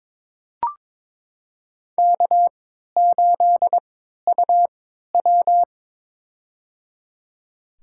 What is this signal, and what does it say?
Morse code, heard as a keyed tone, about 22 wpm, spaced as for 13 wpm, 700 Hz. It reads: K8UW
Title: USA Call Signs - 4 Characters - 3x Character Spacing - 22wpm